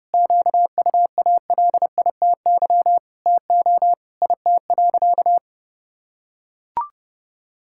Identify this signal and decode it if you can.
Morse code, heard as a keyed tone, about 30 wpm, 700 Hz. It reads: QUALITY TO IT.
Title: Sentences from Top 1000 Words - Review Part 6 - 30wpm